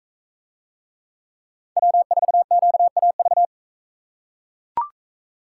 Morse code, heard as a keyed tone, about 42 wpm, 700 Hz. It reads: W4QAV